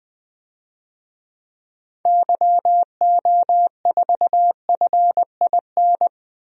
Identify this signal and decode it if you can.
Morse code, heard as a keyed tone, about 20 wpm, 700 Hz. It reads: YO4FIN
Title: DXCC-100 Call Signs - 6 Characters - 20wpm